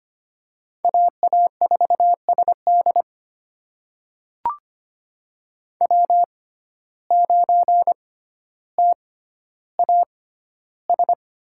Morse code, heard as a keyed tone, about 25 wpm, spaced as for 13 wpm, 700 Hz. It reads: AA4SD W9TAS
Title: USA Call Signs - 5 Characters - 6x Character Spacing - 25wpm